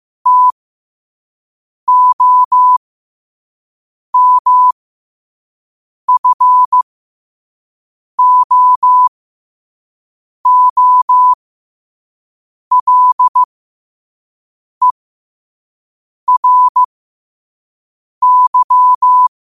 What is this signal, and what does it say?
Morse code, heard as a keyed tone, about 15 wpm, spaced as for 5 wpm, 1000 Hz. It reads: TOMFOOLERY
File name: qotdfiles / 5wpm